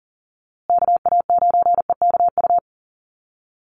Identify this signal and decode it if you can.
Morse code, heard as a keyed tone, about 40 wpm, 700 Hz. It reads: KA9EKU
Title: USA Call Signs Repeated 2x 40wpm